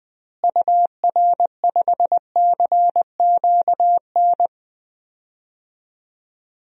Morse code, heard as a keyed tone, about 20 wpm, 700 Hz. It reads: UR5CQN
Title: DXCC-100 Call Signs - 6 Characters - Repeated 3x - 20wpm